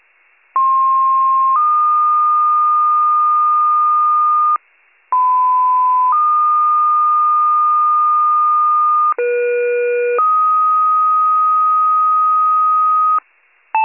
Two-tone page, heard as a one-second tone, then a three-second tone, about 1060 then 1220 Hz, about 1010 then 1220 Hz, about 480 then 1250 Hz. It is heard from 0.6 to 13.2 s.